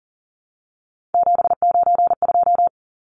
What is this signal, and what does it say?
Morse code, heard as a keyed tone, about 40 wpm, 700 Hz. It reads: 792
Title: Numbers - 3-Digit 40wpm